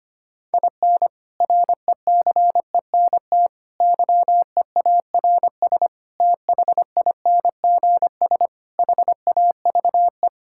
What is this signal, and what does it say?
Morse code, heard as a keyed tone, about 25 wpm, 700 Hz. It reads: IN RECENT YEARS THINGS HAVE
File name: practice / 25wpm / Sentences with 3-Word Phrases - 25wpm